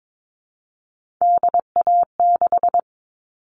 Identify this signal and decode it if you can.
Morse code, heard as a keyed tone, about 22 wpm, 700 Hz. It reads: DA6
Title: All Cut Numbers - 3-Digit 22wpm